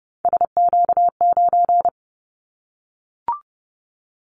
Morse code, heard as a keyed tone, about 30 wpm, 690 Hz. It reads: SQ9
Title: Callsign Prefixes for Top 25 DXCC - Repeated 3x - 30wpm